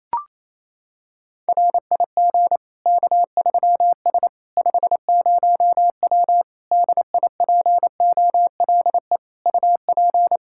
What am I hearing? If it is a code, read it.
Morse code, heard as a keyed tone, about 28 wpm, 700 Hz. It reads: RIG K3S 50W DIPOLE UP
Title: K1USN SST QSO Practice 28wpm